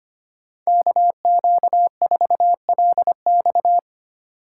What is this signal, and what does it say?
Morse code, heard as a keyed tone, about 25 wpm, 700 Hz. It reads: KQ4LX
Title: USA Call Signs - 5 Characters - 3x Character Spacing - 25wpm